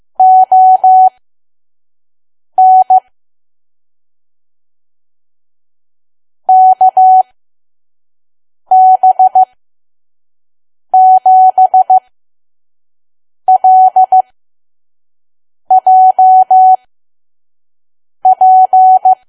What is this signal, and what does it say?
Morse code, heard as a keyed tone, about 15 wpm, spaced as for 5 wpm, 750 Hz. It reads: ON KB7LJP
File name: Archive / 5 WPM files / PM